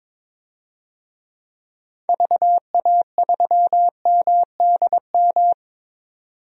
Morse code, heard as a keyed tone, about 22 wpm, 700 Hz. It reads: VA3MDM
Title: DXCC-100 Call Signs - 6 Characters - 22wpm